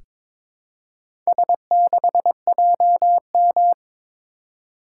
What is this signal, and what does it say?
Morse code, heard as a keyed tone, about 22 wpm, 700 Hz. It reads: S6JM
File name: practice / 22wpm / Four Letter-Number Combinations - 22wpm